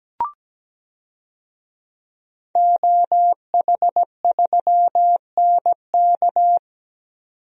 Morse code, heard as a keyed tone, about 17 wpm, 700 Hz. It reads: OH3NK
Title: DXCC-100 Call Signs - 5 Characters - 17wpm